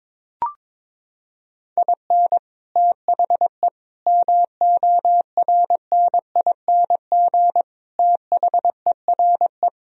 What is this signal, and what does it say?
Morse code, heard as a keyed tone, about 22 wpm, 700 Hz. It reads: IN THE MORNING THERE
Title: Sentences from Top 1000 Words - Review Part 1 - 22wpm